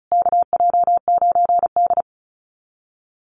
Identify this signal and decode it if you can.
Morse code, heard as a keyed tone, about 34 wpm, 700 Hz. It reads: KJ9D
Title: USA Call Signs Repeated 3x 35wpm